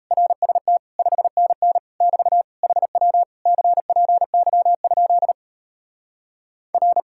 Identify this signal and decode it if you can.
Morse code, heard as a keyed tone, about 37 wpm, 700 Hz. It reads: RST 5NN <BT> HW CPY? R